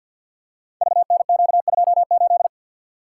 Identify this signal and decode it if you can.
Morse code, heard as a keyed tone, about 48 wpm, 690 Hz. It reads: VNQ28
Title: Code Groups - 5 Letter:Numbers - 50wpm